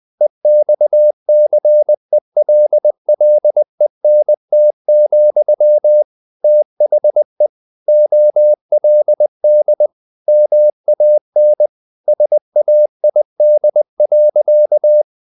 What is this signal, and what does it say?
Morse code, heard as a keyed tone, about 20 wpm, 600 Hz. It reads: EXCELLENT, THE OLD MAN SAID.
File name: the old man and the sea / the old man and the sea part 1 20wpm